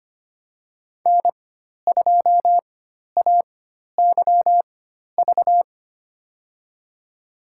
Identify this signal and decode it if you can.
Morse code, heard as a keyed tone, about 25 wpm, spaced as for 12 wpm, 700 Hz. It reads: N2AYV
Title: USA Call Signs - 4x Character Spacing - 25wpm